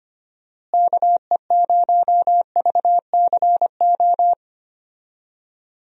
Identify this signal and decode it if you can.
Morse code, heard as a keyed tone, about 25 wpm, 700 Hz. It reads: KE0VCO
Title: USA Call Signs - 6 Characters - Repeated 2x - 25wpm